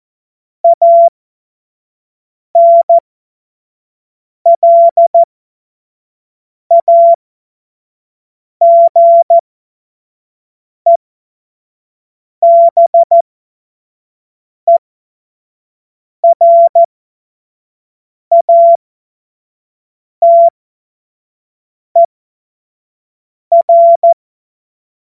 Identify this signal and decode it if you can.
Morse code, heard as a keyed tone, about 14 wpm, spaced as for 5 wpm, 680 Hz. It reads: ANLAGEBERATER
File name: Morsekurs / Witze / Witz 11. April 2023 25 F70